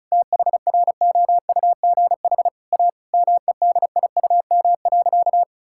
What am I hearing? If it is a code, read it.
Morse code, heard as a keyed tone, about 36 wpm, 700 Hz. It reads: THROUGH A MEDIUM.
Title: Sentences from Top 1000 Words - Review Part 6 - 35wpm